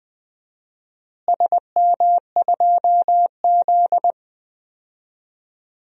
Morse code, heard as a keyed tone, about 20 wpm, 700 Hz. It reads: SM2Z
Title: DXCC-100 Call Signs - 4 Characters - Repeated 3x - 20wpm